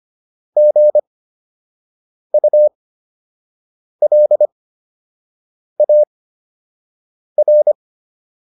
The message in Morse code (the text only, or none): GULAR